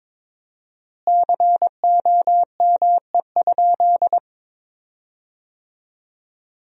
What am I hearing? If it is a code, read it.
Morse code, heard as a keyed tone, about 22 wpm, 700 Hz. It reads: COME?
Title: 4x Word-Spacing - Sentences from Top 200 Words - 22wpm